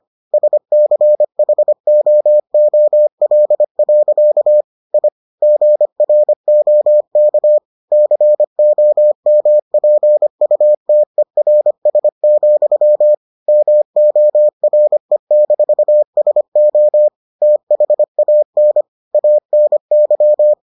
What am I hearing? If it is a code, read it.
Morse code, heard as a keyed tone, about 25 wpm, 600 Hz. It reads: SCHOOL. I GROK COMPUTERS, MORE-SO THAN ANY